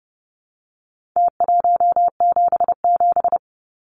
Morse code, heard as a keyed tone, about 30 wpm, 700 Hz. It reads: T177
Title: Numbers - Contest Style - 4-Digit 30wpm